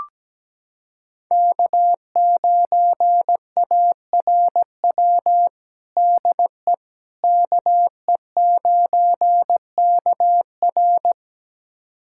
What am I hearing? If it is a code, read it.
Morse code, heard as a keyed tone, about 17 wpm, 700 Hz. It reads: K9ARW DE KE9KR
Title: USA Callsign DE Callsign - 5 Characters - 17wpm